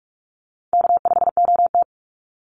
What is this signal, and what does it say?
Morse code, heard as a keyed tone, about 44 wpm, 700 Hz. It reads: K5OT
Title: Call Signs Repeated 2x - 45wpm